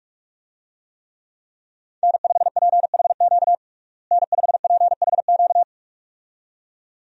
Morse code, heard as a keyed tone, about 41 wpm, 700 Hz. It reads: N5PHQ N5PHQ
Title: USA Call Signs - 5 Characters - Repeated 2x - 45wpm